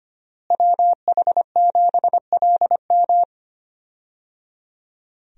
Morse code, heard as a keyed tone, about 25 wpm, 700 Hz. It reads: WH7LM
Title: USA Call Signs - 5 Characters - Repeated 3x - 25wpm